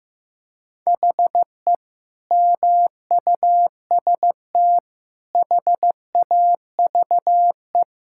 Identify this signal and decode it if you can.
Morse code, heard as a keyed tone, about 15 wpm, 700 Hz. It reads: HE MUST HAVE
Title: Sentences from Top 1000 Words - Review Part 5 - 15wpm